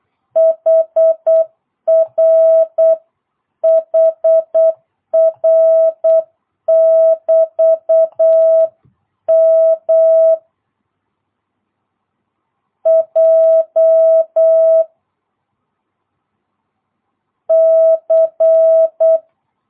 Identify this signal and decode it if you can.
Morse code, heard as a keyed tone, about 8 wpm, 660 Hz. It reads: HR HR <BT> M J C